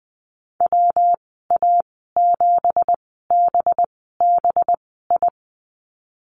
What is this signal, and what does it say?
Morse code, heard as a keyed tone, about 20 wpm, spaced as for 15 wpm, 700 Hz. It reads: WA7BBI
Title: 193 Lesson 31g - Letter v - Call Signs 20wpm